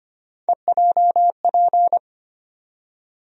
Morse code, heard as a keyed tone, about 25 wpm, 700 Hz. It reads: EJP
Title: USA Call Sign Postfixes - 3 Letters 25wpm